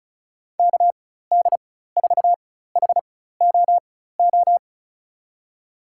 Morse code, heard as a keyed tone, about 36 wpm, spaced as for 16 wpm, 700 Hz. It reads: KD4HOO
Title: USA Call Signs - 6 Characters - 4x Character Spacing - 35wpm